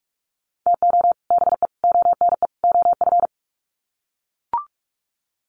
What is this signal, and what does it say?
Morse code, heard as a keyed tone, about 44 wpm, 700 Hz. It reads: TO BE ONE OF
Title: Sets of 4 Words 45wpm